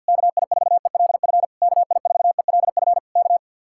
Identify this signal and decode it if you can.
Morse code, heard as a keyed tone, about 66 wpm, spaced as for 58 wpm, 700 Hz. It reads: KI4ELF KI4ELF K
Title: Speed-Racing Common QSO Phrases Encore - 50wpm